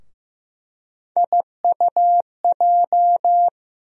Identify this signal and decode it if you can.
Morse code, heard as a keyed tone, about 15 wpm, 700 Hz. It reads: IUJ
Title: USA Call Sign Postfixes - 3 Letters 15wpm